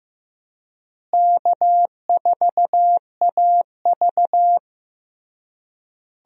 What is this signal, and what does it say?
Morse code, heard as a keyed tone, about 15 wpm, 700 Hz. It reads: K4AV
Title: USA Call Signs - 4 Characters - Repeated 3x - 15wpm